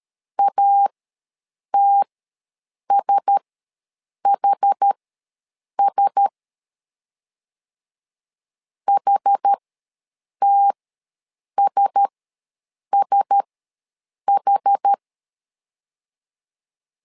Morse code, heard as a keyed tone, about 13 wpm, spaced as for 7 wpm, 800 Hz. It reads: ATSHS HTSSH
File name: LEZIONI / 4 Q1S ws 12wpm 1827